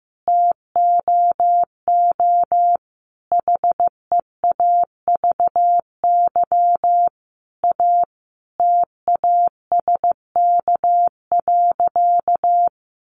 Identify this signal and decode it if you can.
Morse code, heard as a keyed tone, about 15 wpm, 700 Hz. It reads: TOO HEAVY A TASK.
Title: Sentences from Top 1100 Words 15wpm